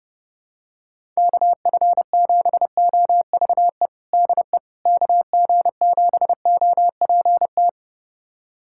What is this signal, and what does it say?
Morse code, heard as a keyed tone, about 32 wpm, 700 Hz. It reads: KF7OVE DE KG7OPT
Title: USA Callsign DE Callsign - 6 Characters - 30wpm